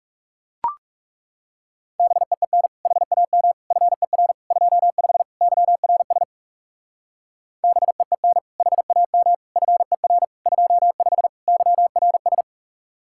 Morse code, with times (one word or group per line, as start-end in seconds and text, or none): 2.0-2.7 BEEN
2.9-3.5 HAM
3.7-4.3 FER
4.5-5.2 25
5.4-6.2 YRS
7.6-8.4 BEEN
8.6-9.3 HAM
9.6-10.2 FER
10.5-11.3 25
11.5-12.4 YRS